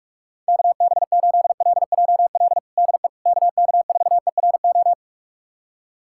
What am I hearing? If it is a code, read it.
Morse code, heard as a keyed tone, about 44 wpm, 700 Hz. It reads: KB8LJL DE KK4ERO
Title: USA Callsign DE Callsign 45wpm